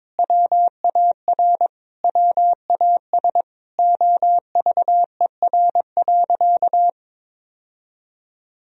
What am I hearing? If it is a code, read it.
Morse code, heard as a keyed tone, about 22 wpm, 700 Hz. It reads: WAR WAS OVER.
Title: Sentences from Top 1000 Words - Review Part 2 - 22wpm